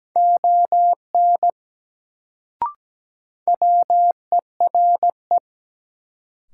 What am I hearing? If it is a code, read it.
Morse code, heard as a keyed tone, about 17 wpm, 700 Hz. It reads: ON WERE
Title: Sentences from Top 100 Words - New Word Drill - 17wpm